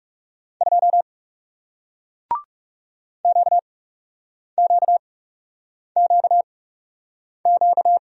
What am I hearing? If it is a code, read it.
Morse code, heard as a keyed tone, about 42 wpm, spaced as for 8 wpm, 700 Hz. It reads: 2 QQQQ